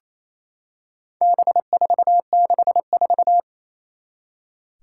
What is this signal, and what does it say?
Morse code, heard as a keyed tone, about 27 wpm, 700 Hz. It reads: B464
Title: Four Challenging Letter-Number Combinations 28wpm